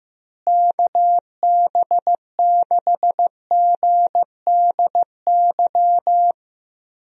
Morse code, heard as a keyed tone, about 15 wpm, 700 Hz. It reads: KB6GDY
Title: USA Call Signs - 6 Characters - Repeated 3x - 15wpm